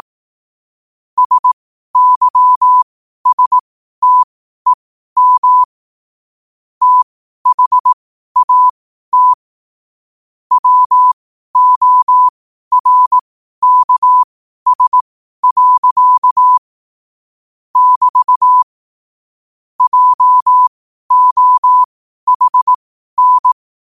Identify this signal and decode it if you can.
Morse code, heard as a keyed tone, about 18 wpm, spaced as for 12 wpm, 1000 Hz. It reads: SYSTEM THAT WORKS. <BT> JOHN
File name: qotdfiles / 12wpm